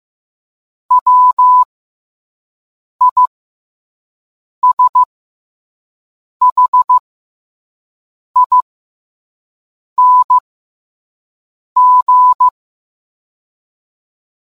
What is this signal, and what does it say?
Morse code, heard as a keyed tone, about 15 wpm, spaced as for 5 wpm, 1000 Hz. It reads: WISHING